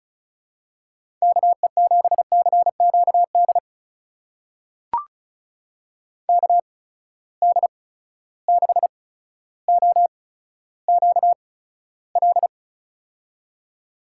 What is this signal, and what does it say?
Morse code, heard as a keyed tone, about 36 wpm, spaced as for 15 wpm, 700 Hz. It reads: KE7CQD KD6OQL
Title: USA Call Signs - 6 Characters - 8x Character Spacing - 35wpm